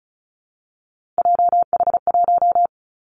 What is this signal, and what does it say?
Morse code, heard as a keyed tone, about 36 wpm, 700 Hz. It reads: JH1
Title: Callsign Prefixes for Top 25 DXCC - 35wpm